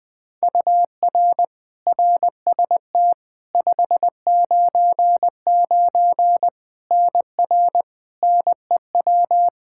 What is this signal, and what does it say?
Morse code, heard as a keyed tone, about 21 wpm, 700 Hz. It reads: UR RST 599 NR NEW
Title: K1USN SST QSO Practice 20wpm